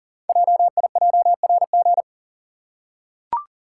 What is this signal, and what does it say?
Morse code, heard as a keyed tone, about 38 wpm, 690 Hz. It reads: JIJRG
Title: Code Groups - 5 Letters - 40wpm